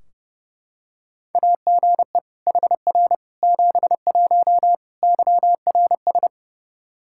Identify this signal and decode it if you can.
Morse code, heard as a keyed tone, about 30 wpm, 700 Hz. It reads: AGE HR 71 YRS
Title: Practice QSO - Exchange 3 - More Info - 30wpm